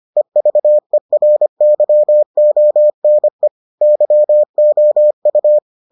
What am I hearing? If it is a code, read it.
Morse code, heard as a keyed tone, about 25 wpm, 600 Hz. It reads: EVERYONE YOU